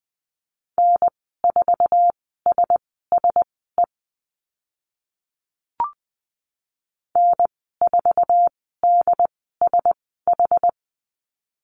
Morse code, heard as a keyed tone, about 20 wpm, spaced as for 15 wpm, 700 Hz. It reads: N4SSE N4DSH
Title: 076 Lesson 13g - Letter L - Call Signs 20wpm